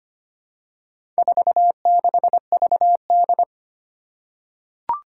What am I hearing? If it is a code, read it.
Morse code, heard as a keyed tone, about 25 wpm, 700 Hz. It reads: 46VD
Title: Four Challenging Letter-Number Combinations 25wpm